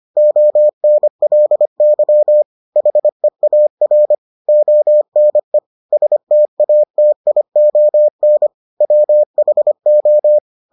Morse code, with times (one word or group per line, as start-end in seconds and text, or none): 0.2-2.4 ONLY
2.8-4.1 HEAR
4.5-5.6 ONE
5.9-8.5 STATION
8.8-10.4 WHO